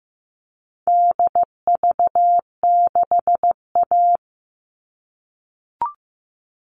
Morse code, heard as a keyed tone, about 15 wpm, 700 Hz. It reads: DV6A
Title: All Cut Numbers - 4-Digit 15wpm